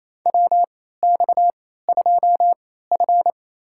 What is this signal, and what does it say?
Morse code, heard as a keyed tone, about 28 wpm, spaced as for 16 wpm, 700 Hz. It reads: WX2F
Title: USA Call Signs - 4 Characters - 3x Character Spacing - 28wpm